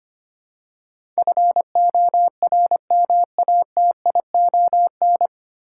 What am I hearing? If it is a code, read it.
Morse code, heard as a keyed tone, about 25 wpm, 690 Hz. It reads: FORMATION
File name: practice / 25wpm / 9 Letter Words - First 3 Letters Given - 25wpm